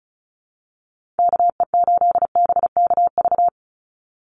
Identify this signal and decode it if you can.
Morse code, heard as a keyed tone, about 36 wpm, 700 Hz. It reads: KE8BKV
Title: USA Call Signs 35wpm